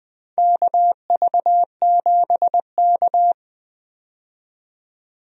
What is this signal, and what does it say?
Morse code, heard as a keyed tone, about 21 wpm, 700 Hz. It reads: KV7K
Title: USA Call Signs - 4 Characters - 20wpm